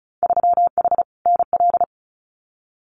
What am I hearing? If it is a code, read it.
Morse code, heard as a keyed tone, about 36 wpm, 700 Hz. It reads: 3H NL
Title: Winter Field Day Exchanges Encore 35wpm